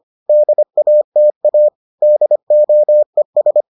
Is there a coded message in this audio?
Yes: Morse code from 0.3 to 3.6 s, heard as a keyed tone, about 25 wpm, 600 Hz. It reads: DATA DOES